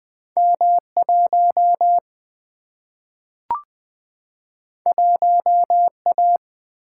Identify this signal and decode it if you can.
Morse code, heard as a keyed tone, about 20 wpm, 700 Hz. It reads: M 1 1 A
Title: Single-Letter Single-Number 20wpm